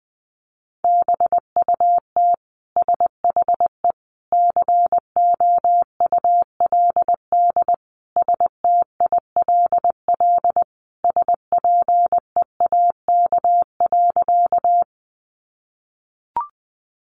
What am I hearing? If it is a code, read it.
Morse code, heard as a keyed tone, about 20 wpm, 700 Hz. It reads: BUT SHE COULD STILL SPEAK.
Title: Sentences from Top 1100 Words 20wpm